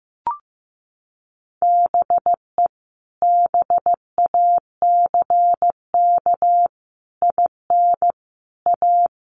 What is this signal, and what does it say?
Morse code, heard as a keyed tone, about 15 wpm, 700 Hz. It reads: BE BACK IN A
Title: Sets of 5 Words 15wpm